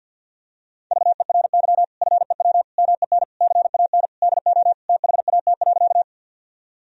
Morse code, heard as a keyed tone, about 50 wpm, 700 Hz. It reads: VERY FEW MEN CAN DO THAT.
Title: Sentences from Top 1000 Words - Review Part 1 - 50wpm